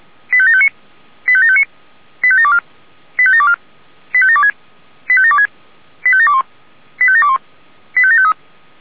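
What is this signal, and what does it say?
Five-tone selective call (ZVEI), heard as quick runs of five tones, tones of about 70 ms. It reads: 85759 85759 85724 85724 85726 85726 85721 85721 85753
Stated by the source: ZVEI five-tone selective call received on an SDR.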